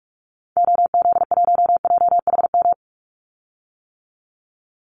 Morse code, heard as a keyed tone, about 44 wpm, 700 Hz. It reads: OZ1JHM
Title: Call Signs - 45wpm